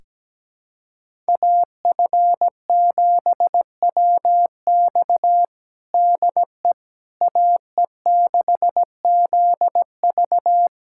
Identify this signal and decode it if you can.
Morse code, heard as a keyed tone, about 17 wpm, 700 Hz. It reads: AF7WX DE AE6ZV
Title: USA Callsign DE Callsign - 5 Characters - 17wpm